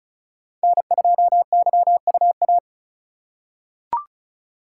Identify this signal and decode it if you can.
Morse code, heard as a keyed tone, about 36 wpm, 700 Hz. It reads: N2YUA
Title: USA Call Signs - 5 Characters - 2x Character Spacing - 35wpm